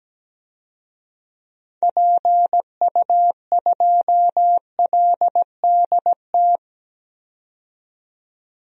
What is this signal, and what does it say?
Morse code, heard as a keyed tone, about 17 wpm, 700 Hz. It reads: PU2LDT
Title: DXCC-100 Call Signs - 6 Characters - 17wpm